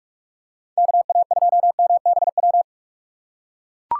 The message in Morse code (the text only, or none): KA2MBW